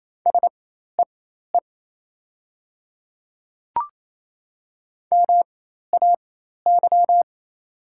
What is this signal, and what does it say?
Morse code, heard as a keyed tone, about 28 wpm, spaced as for 13 wpm, 700 Hz. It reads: SEE MAY
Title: 4x Character Spacing - Top 100 Words - 28wpm